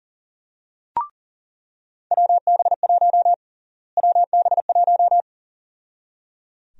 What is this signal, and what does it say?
Morse code, heard as a keyed tone, about 37 wpm, 700 Hz. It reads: WB1 WB1
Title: USA Call Sign Prefixes - Repeated 2x - 40wpm